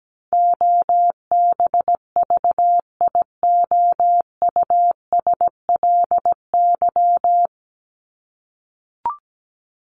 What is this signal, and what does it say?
Morse code, heard as a keyed tone, about 17 wpm, 700 Hz. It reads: OBVIOUSLY
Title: Top 1501-1750 Words 17wpm